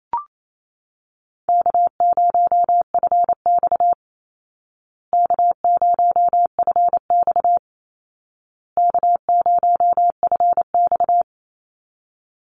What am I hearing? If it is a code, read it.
Morse code, heard as a keyed tone, about 28 wpm, 700 Hz. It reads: K0FX K0FX K0FX
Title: Easy Call Signs Repeated 3x - 28wpm